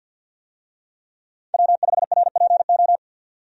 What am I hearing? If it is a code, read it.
Morse code, heard as a keyed tone, about 52 wpm, 700 Hz. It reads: W5RPO
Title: USA Call Signs - 5 Characters - 50wpm